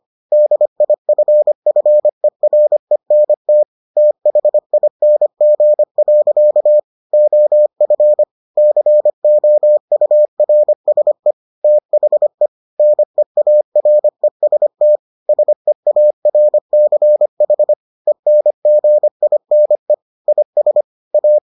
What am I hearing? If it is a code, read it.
Morse code, heard as a keyed tone, about 25 wpm, 600 Hz. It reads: DIFFERENT THING. OF COURSE THE NEAREST SEARCH ENGINE IS A